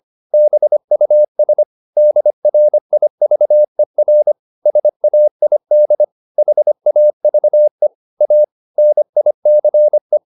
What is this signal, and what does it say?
Morse code, heard as a keyed tone, about 25 wpm, 600 Hz. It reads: BUS DRIVER SAID HAVE A NICE